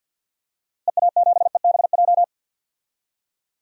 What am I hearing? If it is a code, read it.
Morse code, heard as a keyed tone, about 52 wpm, 700 Hz. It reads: EA7EBJ
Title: DXCC-100 Call Signs - 6 Characters - Repeated 2x - 50wpm